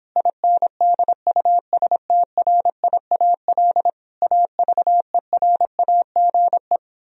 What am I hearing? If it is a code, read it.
Morse code, heard as a keyed tone, about 26 wpm, 700 Hz. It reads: INDUSTRIAL AVERAGE